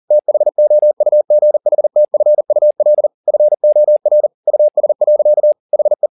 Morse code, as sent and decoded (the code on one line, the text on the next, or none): - .... --- ..- --. .... - ..-. ..- .-.. / ..-. --- .-. / ..- ... .-.-.- / .... .
THOUGHTFUL FOR US. HE